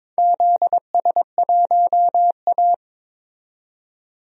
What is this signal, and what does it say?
Morse code, heard as a keyed tone, about 22 wpm, 700 Hz. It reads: ZS1A
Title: DXCC-100 Call Signs - 4 Characters - 22wpm